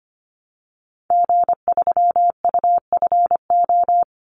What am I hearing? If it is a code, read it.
Morse code, heard as a keyed tone, about 25 wpm, 700 Hz. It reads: G3UFO
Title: Call Signs - 25wpm